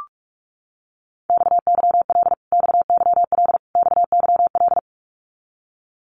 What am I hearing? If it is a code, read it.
Morse code, heard as a keyed tone, about 44 wpm, 700 Hz. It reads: XYL XYL XYL
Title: Abbreviations Repeated 3x - 45wpm